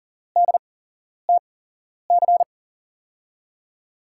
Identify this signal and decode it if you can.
Morse code, heard as a keyed tone, about 40 wpm, spaced as for 11 wpm, 700 Hz. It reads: DTC
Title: USA Call Signs - 6 Characters - 8x Character Spacing - 40wpm